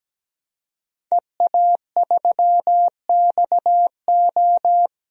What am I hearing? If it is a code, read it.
Morse code, heard as a keyed tone, about 17 wpm, 700 Hz. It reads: EA3XO
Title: DXCC-100 Call Signs - 5 Characters - Repeated 3x - 17wpm